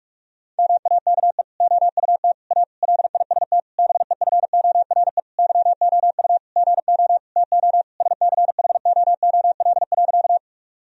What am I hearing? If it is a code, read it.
Morse code, heard as a keyed tone, about 41 wpm, 700 Hz. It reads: MAKE OUT A LIST BEFORE YOU GO TO SCHOOL.
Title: Sentences from Top 1000 Words - Review Part 2 - 45wpm